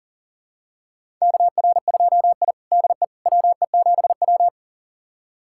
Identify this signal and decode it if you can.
Morse code, heard as a keyed tone, about 40 wpm, 700 Hz. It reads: KR2I DE WE7W
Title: USA Callsign DE Callsign - 4 Characters - 40wpm